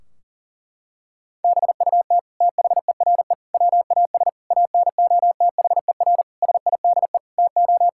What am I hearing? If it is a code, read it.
Morse code, heard as a keyed tone, about 37 wpm, 700 Hz. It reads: BUT THERE WAS ANOTHER SIDE TO
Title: Sentences from Top 1600 Words 40wpm